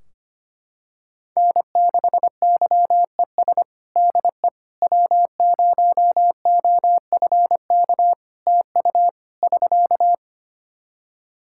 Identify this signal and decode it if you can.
Morse code, heard as a keyed tone, about 25 wpm, 700 Hz. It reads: N6YES DE W0OFK TU <SK>
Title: K1USN SST QSO Practice 25wpm